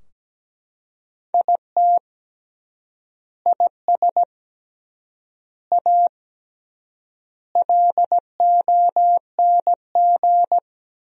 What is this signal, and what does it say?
Morse code, heard as a keyed tone, about 17 wpm, 700 Hz. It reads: IT IS A LONG
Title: 3x Word-Spacing - Sentences from Top 300 Words - 17wpm